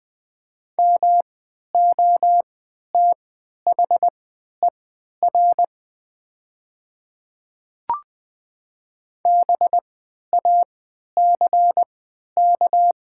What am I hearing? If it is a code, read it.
Morse code, heard as a keyed tone, about 20 wpm, spaced as for 11 wpm, 700 Hz. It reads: MOTHER BACK
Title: 3x Character Spacing - Top 500 Words - 20wpm